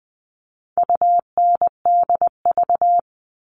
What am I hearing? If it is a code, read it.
Morse code, heard as a keyed tone, about 20 wpm, 700 Hz. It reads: UNDV
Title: All Cut Numbers - 4-Digit 20wpm